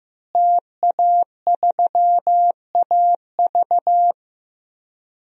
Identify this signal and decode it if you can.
Morse code, heard as a keyed tone, about 15 wpm, 700 Hz. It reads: TA3AV
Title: DXCC-100 Call Signs - 5 Characters - Repeated 2x - 15wpm